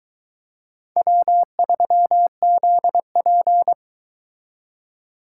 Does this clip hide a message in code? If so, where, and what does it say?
Yes: Morse code from 1.0 to 3.7 s, heard as a keyed tone, about 22 wpm, 700 Hz. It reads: W3ZP